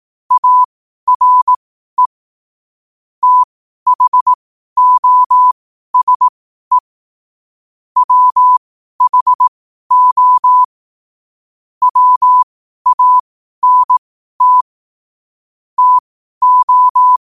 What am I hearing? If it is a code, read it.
Morse code, heard as a keyed tone, about 18 wpm, spaced as for 12 wpm, 1000 Hz. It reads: ARE THOSE WHO WANT TO
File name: qotdfiles / 12wpm